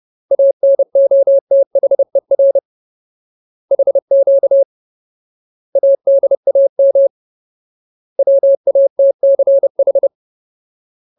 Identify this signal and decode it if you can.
Morse code, heard as a keyed tone, about 30 wpm, 550 Hz. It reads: ANOTHER HQ ADAM WATCH
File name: Ultimate Word List / 30WPM Ultimate